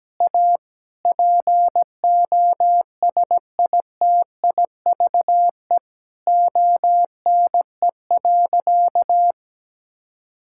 Morse code, heard as a keyed tone, about 17 wpm, 700 Hz. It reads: A POSITIVE ONE.
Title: Sentences from Top 1000 Words - Review Part 6 - 17wpm